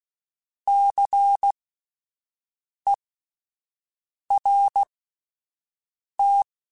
Morse code, heard as a keyed tone, about 16 wpm, spaced as for 5 wpm, 800 Hz. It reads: CERT
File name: Alphabetical / Alphabetical mp3s / Cs @5